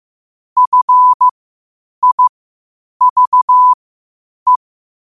Morse code, heard as a keyed tone, about 15 wpm, spaced as for 8 wpm, 1000 Hz. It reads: FIVE
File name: qotdfiles / 8wpm